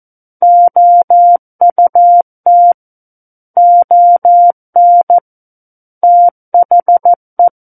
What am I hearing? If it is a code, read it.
Morse code, heard as a keyed tone, about 14 wpm, 700 Hz. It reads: OUT ON THE
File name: Book One / 05-Chapter 5 - 14 WPM